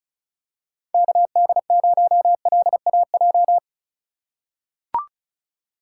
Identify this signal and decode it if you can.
Morse code, heard as a keyed tone, about 36 wpm, 700 Hz. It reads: KD0LAJ
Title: USA Call Signs - 6 Characters - 4x Character Spacing - 35wpm